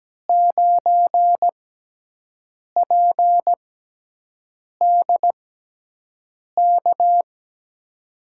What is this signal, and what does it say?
Morse code, heard as a keyed tone, about 17 wpm, spaced as for 6 wpm, 700 Hz. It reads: 9PDK
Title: USA Call Signs - 5 Characters - 6x Character Spacing - 17wpm